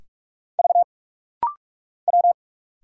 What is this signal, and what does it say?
Morse code, heard as a keyed tone, about 46 wpm, 700 Hz. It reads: V W
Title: Single Letters - Rapid-Fire 45wpm